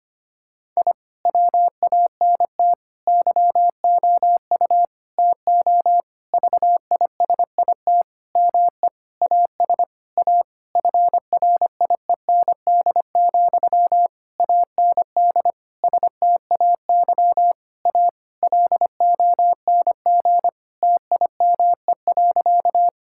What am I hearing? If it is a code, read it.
Morse code, heard as a keyed tone, about 25 wpm, 700 Hz. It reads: I WANT YOU TO VISIT ME AS A FRIEND, AND STAY A LONG TIME.
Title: Sentences from Top 1000 Words - Review Part 6 - 25wpm